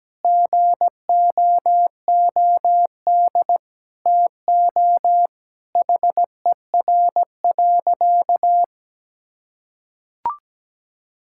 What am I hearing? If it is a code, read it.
Morse code, heard as a keyed tone, about 17 wpm, 700 Hz. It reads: GOOD TO HER.
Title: Sentences from Top 1000 Words - Review Part 5 - 17wpm